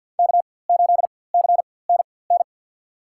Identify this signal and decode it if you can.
Morse code, heard as a keyed tone, about 53 wpm, spaced as for 24 wpm, 700 Hz. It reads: K8CNN
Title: USA Call Signs - 5 Characters - 4x Character Spacing - 50wpm